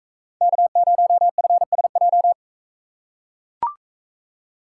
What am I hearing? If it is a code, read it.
Morse code, heard as a keyed tone, about 42 wpm, 700 Hz. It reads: K0FSJ